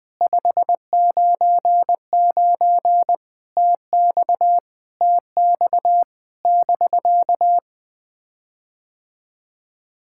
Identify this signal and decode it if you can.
Morse code, heard as a keyed tone, about 20 wpm, 700 Hz. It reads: 599 TX TX <BK>
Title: POTA QSO Practice - 20wpm